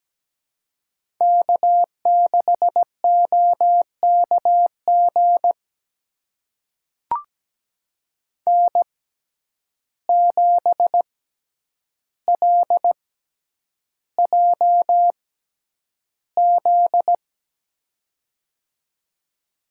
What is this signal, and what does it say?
Morse code, heard as a keyed tone, about 17 wpm, spaced as for 9 wpm, 700 Hz. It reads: K6OKG N7LJZ
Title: USA Call Signs - 5 Characters - 6x Character Spacing - 17wpm